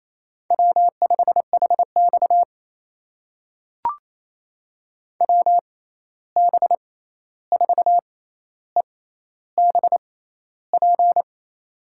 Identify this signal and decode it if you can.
Morse code, heard as a keyed tone, about 28 wpm, spaced as for 13 wpm, 700 Hz. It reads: W5HX WB4EBP
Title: USA Call Signs - 6x Character Spacing - 28wpm